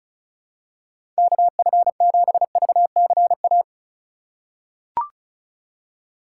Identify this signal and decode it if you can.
Morse code, heard as a keyed tone, about 36 wpm, 700 Hz. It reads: KF7VCA
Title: USA Call Signs - 3x Character Spacing - 35wpm